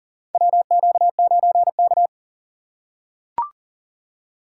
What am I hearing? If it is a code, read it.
Morse code, heard as a keyed tone, about 40 wpm, 700 Hz. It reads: WQ9K
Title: USA Call Signs - 4 Characters - 6x Character Spacing - 40wpm